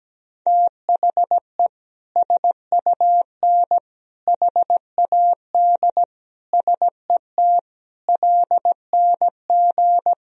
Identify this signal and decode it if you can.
Morse code, heard as a keyed tone, about 17 wpm, 700 Hz. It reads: THE SUN HAD SET LNG
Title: Sentences from Top 300 Words and Abbreviations 17wpm